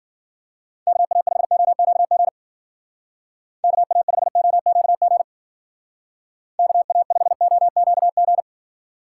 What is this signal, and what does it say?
Morse code, heard as a keyed tone, about 48 wpm, 700 Hz. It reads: KA5OQG KA5OQG KA5OQG
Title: Speed Racing - USA Call Signs 40wpm